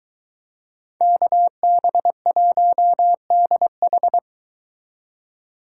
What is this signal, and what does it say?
Morse code, heard as a keyed tone, about 23 wpm, 700 Hz. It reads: KB1DH